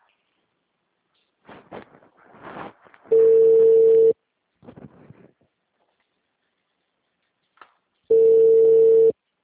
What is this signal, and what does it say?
Ringback tone (China, 450 Hz).
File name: out- -2015- - - .12892